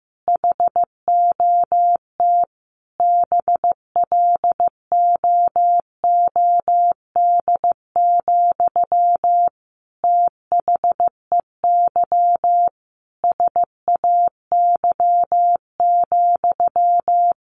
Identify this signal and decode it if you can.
Morse code, heard as a keyed tone, about 15 wpm, 700 Hz. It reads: HOT BLOOD, THEY SAY,
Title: Sentences from Top 1100 Words 15wpm